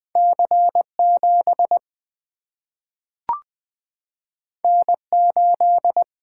Morse code, heard as a keyed tone, about 20 wpm, 700 Hz. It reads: C7 N8
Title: Single-Letter Single-Number 20wpm